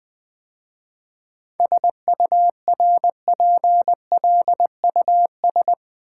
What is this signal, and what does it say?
Morse code, heard as a keyed tone, about 20 wpm, 700 Hz. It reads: SURPLUS
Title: 7 Letter Words - First 3 Letters Given - 20wpm